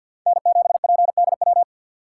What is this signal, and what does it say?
Morse code, heard as a keyed tone, about 48 wpm, 700 Hz. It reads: N7PDW
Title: USA Call Signs - 3x Character Spacing - 50wpm